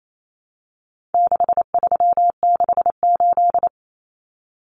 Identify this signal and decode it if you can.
Morse code, heard as a keyed tone, about 28 wpm, 700 Hz. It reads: 6368
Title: Common Cut Numbers - 4-Digit 28wpm